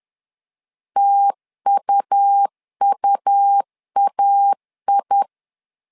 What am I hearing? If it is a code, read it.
Morse code, heard as a keyed tone, about 10 wpm, 800 Hz. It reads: TUUAI
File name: LEZIONI / 2 UHDN / 2 UH 10wpm 1714